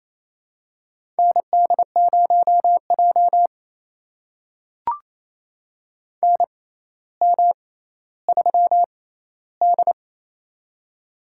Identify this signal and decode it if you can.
Morse code, heard as a keyed tone, about 28 wpm, spaced as for 14 wpm, 700 Hz. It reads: ND0J NM3D
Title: USA Call Signs - 4 Characters - 6x Character Spacing - 28wpm